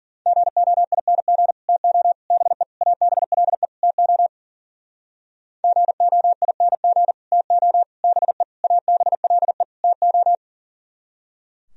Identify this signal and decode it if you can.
Morse code, heard as a keyed tone, about 50 wpm, 700 Hz. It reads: GOING TO BE ABLE TO GOING TO BE ABLE TO
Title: Speed-Racing - Sets of 5 Words 40wpm